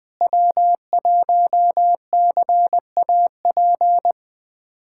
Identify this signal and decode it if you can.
Morse code, heard as a keyed tone, about 20 wpm, 700 Hz. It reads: W1CAP
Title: USA Callsign DE Callsign - 5 Characters - 20wpm